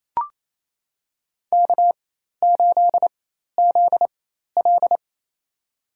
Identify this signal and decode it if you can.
Morse code, heard as a keyed tone, about 28 wpm, spaced as for 13 wpm, 700 Hz. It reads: K8ZL
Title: USA Call Signs - 4x Character Spacing - 28wpm